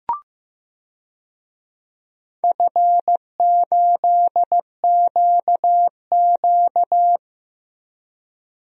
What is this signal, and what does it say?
Morse code, heard as a keyed tone, about 15 wpm, 700 Hz. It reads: F8QQ
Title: DXCC-100 Call Signs - 4 Characters - Repeated 3x - 15wpm